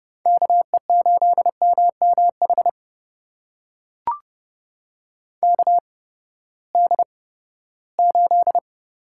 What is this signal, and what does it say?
Morse code, heard as a keyed tone, about 30 wpm, spaced as for 17 wpm, 700 Hz. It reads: KE8MMH KD8
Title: USA Call Signs - 6 Characters - 8x Character Spacing - 30wpm